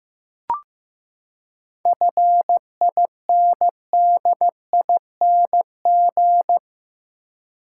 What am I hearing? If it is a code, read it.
Morse code, heard as a keyed tone, about 15 wpm, 700 Hz. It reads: FINDING
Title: Seven Letter Words - 15wpm